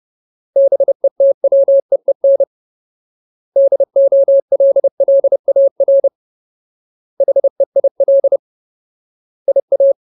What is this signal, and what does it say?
Morse code, heard as a keyed tone, about 30 wpm, 550 Hz. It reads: BETWEEN DOLLAR HEIL IA
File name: Ultimate Word List / 30WPM Ultimate